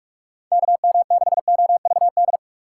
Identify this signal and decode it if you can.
Morse code, heard as a keyed tone, about 42 wpm, 700 Hz. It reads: KM6OVD
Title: USA Call Signs - 4x Character Spacing - 45wpm